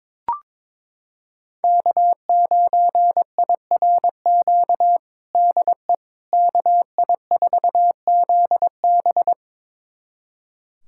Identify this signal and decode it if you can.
Morse code, heard as a keyed tone, about 22 wpm, 700 Hz. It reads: K9IRQ DE KI4ZB
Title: USA Callsign DE Callsign - 5 Characters - 22wpm